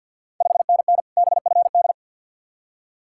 Morse code, heard as a keyed tone, about 53 wpm, 690 Hz. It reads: 5NN BFD
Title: Ontario QSO Party Exchange 50wpm